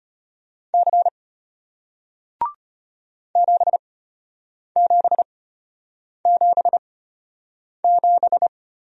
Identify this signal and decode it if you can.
Morse code, heard as a keyed tone, about 33 wpm, spaced as for 8 wpm, 700 Hz. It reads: C 7777